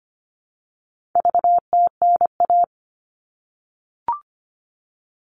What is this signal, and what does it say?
Morse code, heard as a keyed tone, about 25 wpm, 700 Hz. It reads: VTNA
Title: All Cut Numbers - 4-Digit 25wpm